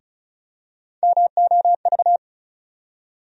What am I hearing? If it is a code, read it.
Morse code, heard as a keyed tone, about 36 wpm, 700 Hz. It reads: MOV
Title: USA Call Sign Postfixes - 3 Letters 35wpm